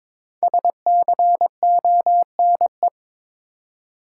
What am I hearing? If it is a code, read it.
Morse code, heard as a keyed tone, about 21 wpm, 700 Hz. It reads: SCONE
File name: Difficult Words with Hints - 22wpm